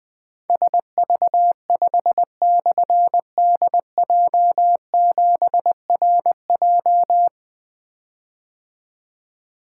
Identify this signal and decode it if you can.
Morse code, heard as a keyed tone, about 20 wpm, 700 Hz. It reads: SV5/DJ7RJ
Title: Difficult Call Signs 20wpm